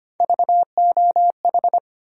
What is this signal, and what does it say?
Morse code, heard as a keyed tone, about 25 wpm, 700 Hz. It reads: VOH
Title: USA Call Sign Postfixes - 3 Letters 25wpm